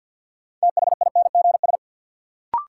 Morse code, heard as a keyed tone, about 51 wpm, 700 Hz. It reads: THINGS
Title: Sets of 3 Words - New Word Drill - 50wpm